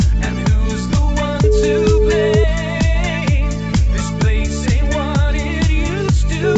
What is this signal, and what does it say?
Ringback tone (Europe, 425 Hz), with noise.